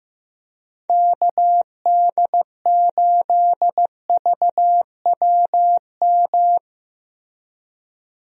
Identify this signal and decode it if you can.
Morse code, heard as a keyed tone, about 15 wpm, 700 Hz. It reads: KD8VWM
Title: USA Call Signs - 6 Characters - Repeated 2x - 15wpm